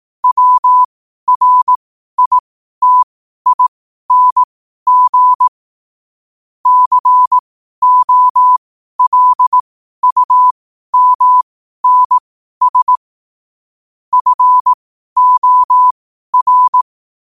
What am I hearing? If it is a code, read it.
Morse code, heard as a keyed tone, about 18 wpm, spaced as for 12 wpm, 1000 Hz. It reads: WRITING COLUMNS FOR
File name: qotdfiles / 12wpm